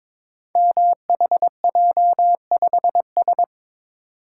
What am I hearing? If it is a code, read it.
Morse code, heard as a keyed tone, about 22 wpm, 700 Hz. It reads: MHJ5S
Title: Code Groups - 5 Letter:Numbers - 22wpm